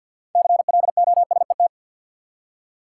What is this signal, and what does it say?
Morse code, heard as a keyed tone, about 48 wpm, 700 Hz. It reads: CLOSET
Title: Six Letter Words - 50wpm